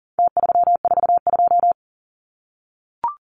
Morse code, heard as a keyed tone, about 40 wpm, 700 Hz. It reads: T342
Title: Numbers - Contest Style - 4-Digit 40wpm